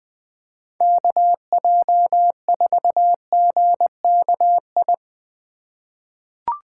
Morse code, heard as a keyed tone, about 20 wpm, 700 Hz. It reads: KJ4GKI
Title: USA Call Signs - 6 Characters - 4x Character Spacing - 20wpm